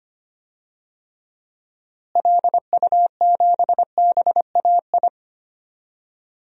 Morse code, heard as a keyed tone, about 25 wpm, 700 Hz. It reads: LU7BAI
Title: DXCC-100 Call Signs - 6 Characters - Repeated 2x - 25wpm